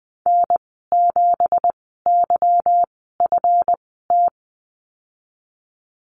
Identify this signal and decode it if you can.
Morse code, heard as a keyed tone, about 20 wpm, spaced as for 15 wpm, 700 Hz. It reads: N7YFT
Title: 173 Lesson 28g - Number 9 - Call Signs 20wpm